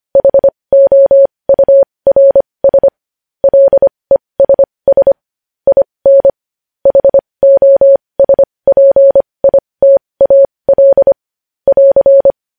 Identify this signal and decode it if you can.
Morse code, heard as a keyed tone, about 25 wpm, spaced as for 20 wpm, 550 Hz. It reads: HOURS LESS IN HOSPITAL <AR>
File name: podcast / News headlines-1x-25x20